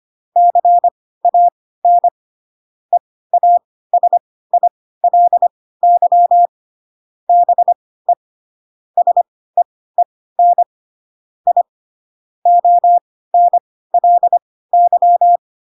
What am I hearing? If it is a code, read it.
Morse code, heard as a keyed tone, about 25 wpm, spaced as for 16 wpm, 700 Hz. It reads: CAN EASILY BE SEEN I ONLY
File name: Pinocchio / Pinocchio ch1 10 25 ik1hge